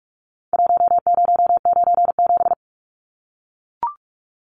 Morse code, heard as a keyed tone, about 44 wpm, 700 Hz. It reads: 1097
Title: Numbers - Contest Style - 4-Digit 45wpm